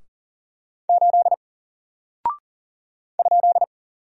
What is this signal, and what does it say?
Morse code, heard as a keyed tone, about 41 wpm, 700 Hz. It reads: 8 ?
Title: Single Character (Letter:Number:Punctuation:Prosign) Alt 40wpm